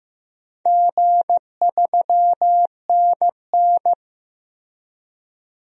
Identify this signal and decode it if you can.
Morse code, heard as a keyed tone, about 15 wpm, 700 Hz. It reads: G3NN
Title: DXCC-100 Call Signs - 4 Characters - Repeated 2x - 15wpm